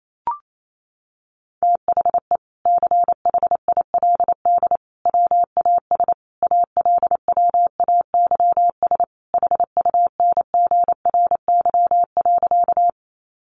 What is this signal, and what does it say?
Morse code, heard as a keyed tone, about 28 wpm, 700 Hz. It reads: THE CHILD WAS ALWAYS HUNGRY.
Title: Sentences from Top 1400 Words 28wpm